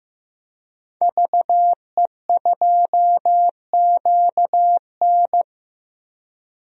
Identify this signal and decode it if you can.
Morse code, heard as a keyed tone, about 15 wpm, 700 Hz. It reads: VE2QN
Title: DXCC-100 Call Signs - 5 Characters - 15wpm